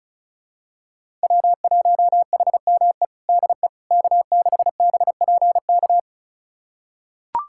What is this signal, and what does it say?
Morse code, heard as a keyed tone, about 36 wpm, 700 Hz. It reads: W1HME DE K6BPK
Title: USA Callsign DE Callsign - 5 Characters - 35wpm